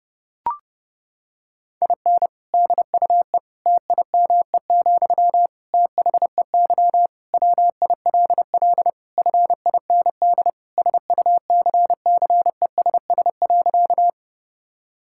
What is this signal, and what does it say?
Morse code, heard as a keyed tone, about 32 wpm, 700 Hz. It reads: IN DUE TIME, THEY WILL FIND SUCCESS.
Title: Sentences from Top 1000 Words - Review Part 6 - 30wpm